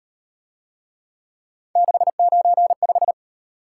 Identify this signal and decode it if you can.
Morse code, heard as a keyed tone, about 37 wpm, 700 Hz. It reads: 695